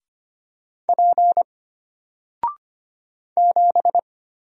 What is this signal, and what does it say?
Morse code, heard as a keyed tone, about 25 wpm, 700 Hz. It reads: P 7
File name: Single-Letter-Number-25wpm